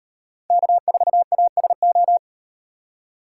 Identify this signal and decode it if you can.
Morse code, heard as a keyed tone, about 38 wpm, 700 Hz. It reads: K4ASO